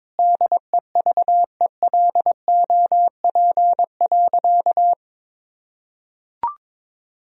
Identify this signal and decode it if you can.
Morse code, heard as a keyed tone, about 22 wpm, 700 Hz. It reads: DEVELOP.
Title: Sentences from Top 1000 Words - Review Part 2 - 22wpm